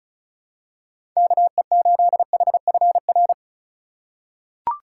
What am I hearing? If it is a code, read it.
Morse code, heard as a keyed tone, about 34 wpm, 700 Hz. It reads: KE8HFR
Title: USA Call Signs - 4x Character Spacing - 35wpm